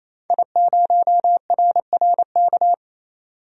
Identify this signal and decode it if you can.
Morse code, heard as a keyed tone, about 28 wpm, 700 Hz. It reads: I0RRK
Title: DXCC-100 Call Signs - 5 Characters - 28wpm